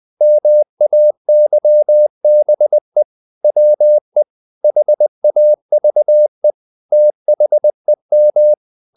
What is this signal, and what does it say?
Morse code, heard as a keyed tone, about 20 wpm, 600 Hz. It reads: MAYBE WE HAVE THEM
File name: the old man and the sea part 2 20wpm